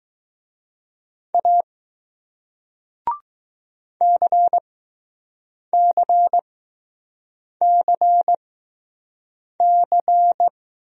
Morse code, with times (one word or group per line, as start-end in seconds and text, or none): 1.3-1.6 A
4.0-10.5 CCCC